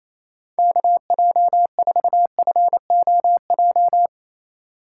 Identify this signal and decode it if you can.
Morse code, heard as a keyed tone, about 35 wpm, 700 Hz. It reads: KJ4FOJ